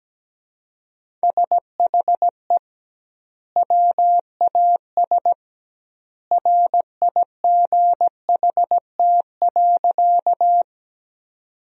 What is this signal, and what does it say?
Morse code, heard as a keyed tone, about 17 wpm, 700 Hz. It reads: SHE WAS RIGHT.
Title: 2x Word-Spacing - Sentences from Top 200 Words - 17wpm